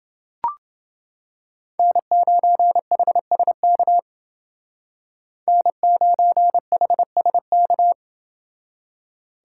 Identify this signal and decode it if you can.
Morse code, heard as a keyed tone, about 29 wpm, 700 Hz. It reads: N9HSK N9HSK